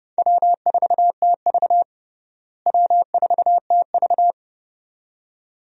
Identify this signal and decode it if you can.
Morse code, heard as a keyed tone, about 30 wpm, 700 Hz. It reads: W4TV W4TV
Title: USA Call Signs - 4 Characters - Repeated 2x - 30wpm